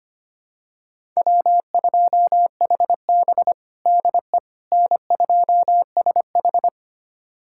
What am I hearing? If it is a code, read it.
Morse code, heard as a keyed tone, about 25 wpm, 700 Hz. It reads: W2HB DE N2SH
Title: USA Callsign DE Callsign - 4 Characters - 25wpm